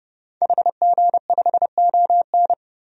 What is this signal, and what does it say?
Morse code, heard as a keyed tone, about 30 wpm, 700 Hz. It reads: HG5ON
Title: DXCC-100 Call Signs - 5 Characters - 30wpm